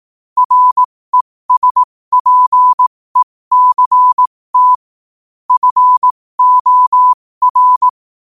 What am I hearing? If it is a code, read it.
Morse code, heard as a keyed tone, about 18 wpm, spaced as for 15 wpm, 1000 Hz. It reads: RESPECT FOR
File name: qotdfiles / 15wpm